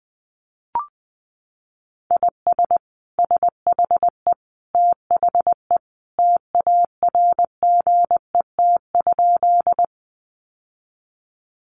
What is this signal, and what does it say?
Morse code, heard as a keyed tone, about 20 wpm, 700 Hz. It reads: IS SHE THE TARGET?
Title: Sentences from Top 1200 Words 20wpm